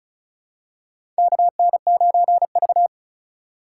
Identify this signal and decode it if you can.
Morse code, heard as a keyed tone, about 34 wpm, 700 Hz. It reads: KN9V
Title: USA Call Signs - 3x Character Spacing - 35wpm